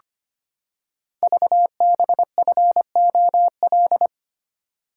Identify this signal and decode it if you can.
Morse code, heard as a keyed tone, about 25 wpm, 700 Hz. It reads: VBFOL
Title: Code Groups - 5 Letters - 25wpm